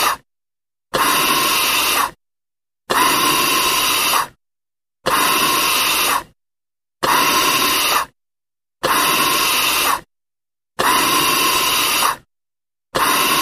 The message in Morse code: none